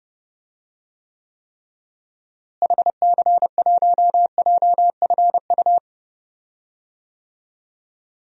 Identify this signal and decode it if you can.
Morse code, heard as a keyed tone, about 30 wpm, 700 Hz. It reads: HC1JFU
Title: DXCC-100 Call Signs - 6 Characters - Repeated 3x - 30wpm